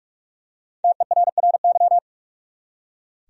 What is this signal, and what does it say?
Morse code, heard as a keyed tone, about 42 wpm, 700 Hz. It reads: TERRY
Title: USA Common Names - 45wpm